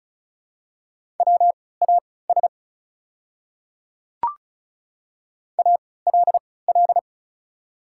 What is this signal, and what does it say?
Morse code, heard as a keyed tone, about 36 wpm, spaced as for 20 wpm, 700 Hz. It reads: WAS ALL
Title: 3x Character Spacing - Top 300 Words - 35wpm